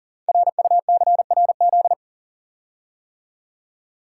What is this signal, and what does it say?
Morse code, heard as a keyed tone, about 42 wpm, 690 Hz. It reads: RUCRZ
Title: Code Groups - 5 Letters - 40wpm